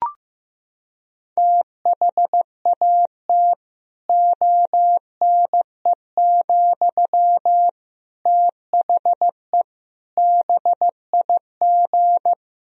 Morse code, heard as a keyed tone, about 16 wpm, 700 Hz. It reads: THAT ONE, THE BIG
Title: Sentences from Top 1000 Words - Review Part 1 - 15wpm